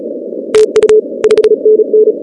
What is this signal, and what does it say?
Morse code, heard as a keyed tone, about 36 wpm, 430 Hz. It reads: TU 5NN
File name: contests / 2019-cw-wpx-kt5j-k5pi- -2330 0-16